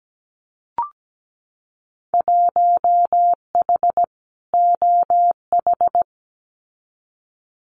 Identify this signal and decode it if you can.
Morse code, heard as a keyed tone, about 17 wpm, 700 Hz. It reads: 1H OH
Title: Winter Field Day Exchanges 17wpm